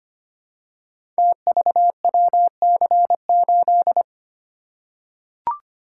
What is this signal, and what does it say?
Morse code, heard as a keyed tone, about 25 wpm, 700 Hz. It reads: TVWC8
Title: Code Groups - 5 Letter:Numbers - 25wpm